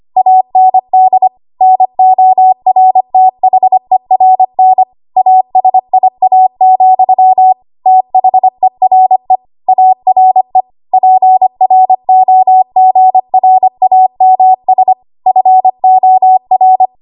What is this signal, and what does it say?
Morse code, heard as a keyed tone, about 25 wpm, 750 Hz. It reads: AND NORTHERN ASIA, THERE ARE PROGRAMS FOR
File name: Archive / 25 WPM / 25WPM